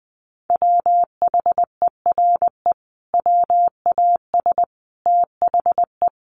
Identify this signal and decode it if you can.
Morse code, heard as a keyed tone, about 20 wpm, 700 Hz. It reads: WHERE WAS THE
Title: Sentences from Top 1000 Words 20wpm